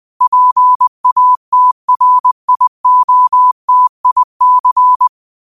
Morse code, heard as a keyed tone, about 20 wpm, 1000 Hz. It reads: PATRIOTIC
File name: qotdfiles / 20wpm